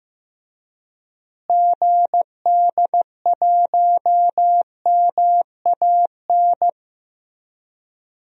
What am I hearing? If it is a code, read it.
Morse code, heard as a keyed tone, about 15 wpm, 700 Hz. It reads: GD1MAN
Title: DXCC-100 Call Signs - 6 Characters - Repeated 3x - 15wpm